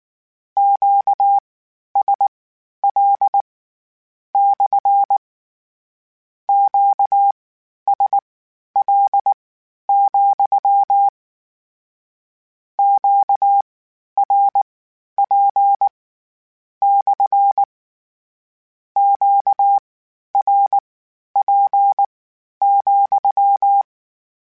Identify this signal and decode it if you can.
Morse code, heard as a keyed tone, about 19 wpm, spaced as for 11 wpm, 800 Hz. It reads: QSL / QSL, QRP / QRP,
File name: Q Calls / Q Calls-mp3s / Q CALLS SPSL @18